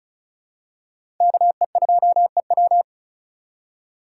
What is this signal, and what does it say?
Morse code, heard as a keyed tone, about 36 wpm, 700 Hz. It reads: KE2EW
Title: USA Call Signs - 2x Character Spacing - 35wpm